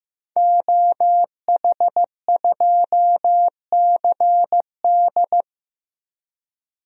Morse code, heard as a keyed tone, about 16 wpm, 700 Hz. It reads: OH2CD
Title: DXCC-100 Call Signs - 5 Characters - Repeated 2x - 15wpm